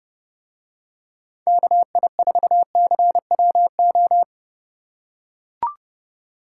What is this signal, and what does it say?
Morse code, heard as a keyed tone, about 32 wpm, 700 Hz. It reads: KI4CWO